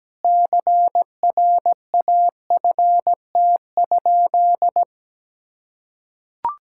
Morse code, heard as a keyed tone, about 17 wpm, 700 Hz. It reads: CRAFT?
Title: Sentences from Top 1000 Words - Review Part 6 - 17wpm